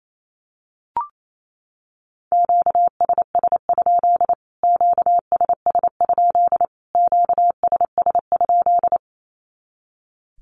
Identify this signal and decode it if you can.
Morse code, heard as a keyed tone, about 28 wpm, 700 Hz. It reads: QSS? QSS? QSS?
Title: Q-Codes Repeated 3x 28wpm